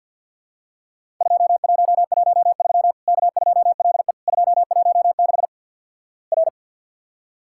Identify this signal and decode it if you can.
Morse code, heard as a keyed tone, about 50 wpm, 700 Hz. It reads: 2113 K2LE 216 R
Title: Worked All Europe DX Contest - QTC Exchange - 50wpm